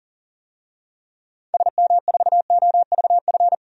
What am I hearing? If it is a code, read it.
Morse code, heard as a keyed tone, about 40 wpm, 700 Hz. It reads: SM4OVF
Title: DXCC-100 Call Signs - 6 Characters - 40wpm